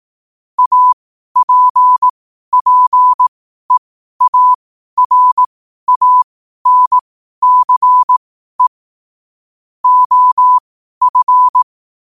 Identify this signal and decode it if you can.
Morse code, heard as a keyed tone, about 18 wpm, spaced as for 12 wpm, 1000 Hz. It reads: APPEARANCE OF
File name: qotdfiles / 12wpm